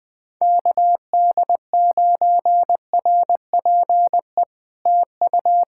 Morse code, heard as a keyed tone, about 20 wpm, 700 Hz. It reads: KD9RPE TU
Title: POTA QSO Practice - 20wpm